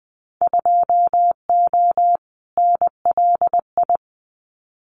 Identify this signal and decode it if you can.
Morse code, heard as a keyed tone, about 20 wpm, 700 Hz. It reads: 2O NLI
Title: Winter Field Day Exchanges 20wpm